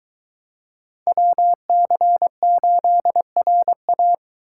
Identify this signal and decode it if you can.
Morse code, heard as a keyed tone, about 20 wpm, 700 Hz. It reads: WC8RA